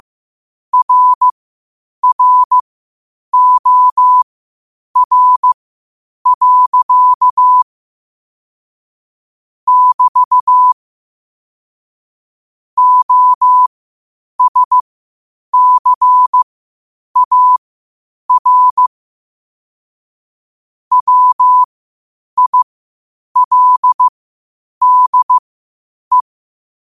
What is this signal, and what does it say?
Morse code, heard as a keyed tone, about 15 wpm, spaced as for 8 wpm, 1000 Hz. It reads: RROR. <BT> OSCAR WILDE